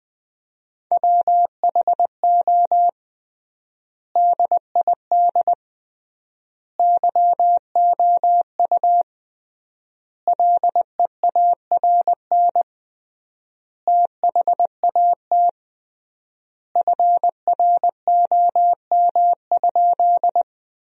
Morse code, heard as a keyed tone, about 21 wpm, 700 Hz. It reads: WHO DID YOU LEARN THAT FROM?
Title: 3x Word-Spacing - Sentences from Top 300 Words - 20wpm